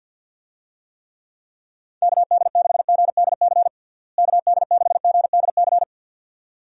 Morse code, heard as a keyed tone, about 51 wpm, 700 Hz. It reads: KD6GDC KD6GDC
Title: USA Call Signs - 6 Characters - Repeated 2x - 50wpm